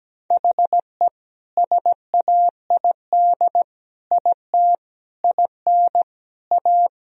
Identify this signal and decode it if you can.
Morse code, heard as a keyed tone, about 17 wpm, 700 Hz. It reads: HE SAID IT IN A
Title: Sentences from Top 1000 Words - Review Part 1 - 17wpm